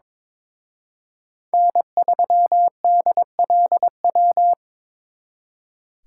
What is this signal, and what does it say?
Morse code, heard as a keyed tone, about 26 wpm, 700 Hz. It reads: N3DLW
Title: Speed Racing - USA 1x3 Call Signs 22wpm